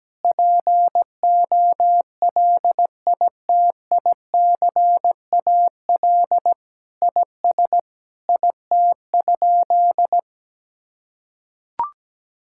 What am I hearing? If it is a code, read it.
Morse code, heard as a keyed tone, about 17 wpm, 700 Hz. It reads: POLITICAL IS IT?
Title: Sentences from Top 1000 Words - Review Part 5 - 17wpm